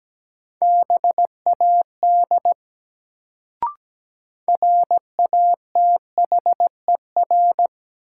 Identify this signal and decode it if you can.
Morse code, heard as a keyed tone, about 17 wpm, 700 Hz. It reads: BAD RATHER
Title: Sentences from Top 700 Words - New Word Drill - 17wpm